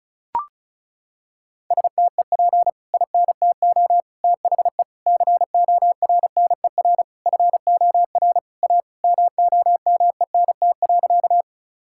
Morse code, heard as a keyed tone, about 34 wpm, 700 Hz. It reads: STEP INTO THE CORNER FOR A MOMENT.
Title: Sentences from Top 1600 Words 35wpm